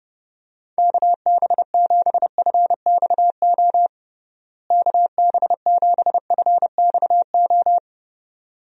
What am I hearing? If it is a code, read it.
Morse code, heard as a keyed tone, about 31 wpm, 700 Hz. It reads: KB7FXO KB7FXO
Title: USA Call Signs - 6 Characters - Repeated 2x - 30wpm